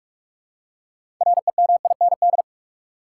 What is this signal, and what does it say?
Morse code, heard as a keyed tone, about 42 wpm, 700 Hz. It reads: REMIND
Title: Six Letter Words - 45wpm